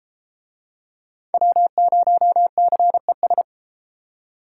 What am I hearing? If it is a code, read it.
Morse code, heard as a keyed tone, about 30 wpm, 700 Hz. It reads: W0CES